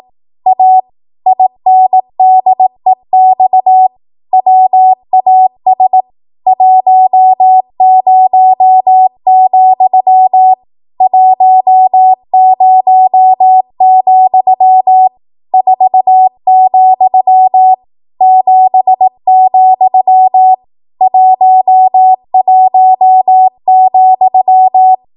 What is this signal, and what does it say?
Morse code, heard as a keyed tone, about 18 wpm, 750 Hz. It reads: A INDEX WAS 10, 10, 4, 7, 11,
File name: Archive / 18 WPM / 18WPM